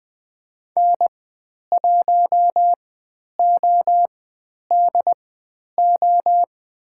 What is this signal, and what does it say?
Morse code, heard as a keyed tone, about 20 wpm, spaced as for 10 wpm, 700 Hz. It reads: N1ODO
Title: 095 Lesson 16g - Letter c - Call Signs 20wpm